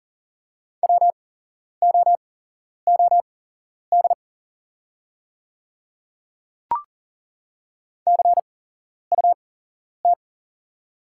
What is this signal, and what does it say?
Morse code, heard as a keyed tone, about 38 wpm, spaced as for 11 wpm, 700 Hz. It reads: WOOD CUT
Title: 8x Character Spacing - Top 300 Words - 40wpm